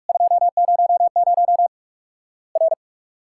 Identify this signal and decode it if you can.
Morse code, heard as a keyed tone, about 46 wpm, 700 Hz. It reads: 200 R
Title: Worked All Europe DX Contest - QTC Exchange - 45wpm